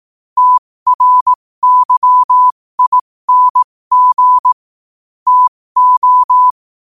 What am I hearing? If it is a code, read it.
Morse code, heard as a keyed tone, about 18 wpm, spaced as for 15 wpm, 1000 Hz. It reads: TRYING TO
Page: 15 WPM morse code quotes